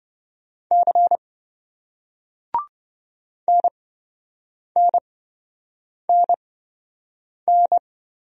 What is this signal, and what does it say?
Morse code, heard as a keyed tone, about 27 wpm, spaced as for 7 wpm, 700 Hz. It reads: C NNNN